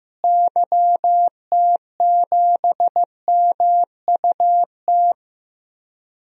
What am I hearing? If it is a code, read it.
Morse code, heard as a keyed tone, about 16 wpm, 700 Hz. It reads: YT7MUT
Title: DXCC-100 Call Signs - 6 Characters - Repeated 3x - 15wpm